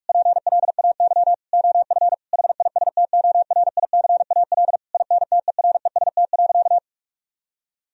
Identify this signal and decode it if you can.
Morse code, heard as a keyed tone, about 46 wpm, 700 Hz. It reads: PLAY OF HISTORICAL INTEREST.
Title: Sentences from Top 1000 Words - Review Part 4 - 45wpm